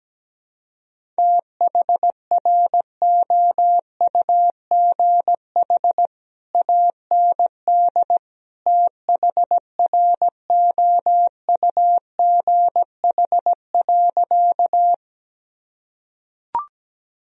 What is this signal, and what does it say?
Morse code, heard as a keyed tone, about 17 wpm, 700 Hz. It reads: THROUGH AND THROUGH.
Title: Sentences from Top 1000 Words - Review Part 1 - 17wpm